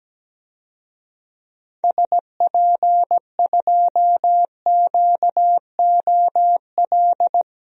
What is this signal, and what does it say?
Morse code, heard as a keyed tone, about 17 wpm, 700 Hz. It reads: SP2QOL
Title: DXCC-100 Call Signs - 6 Characters - 17wpm